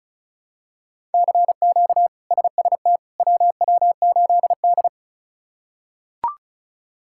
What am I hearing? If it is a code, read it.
Morse code, heard as a keyed tone, about 36 wpm, 700 Hz. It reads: CQ SST WW8D
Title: K1USN SST QSO Practice - 35wpm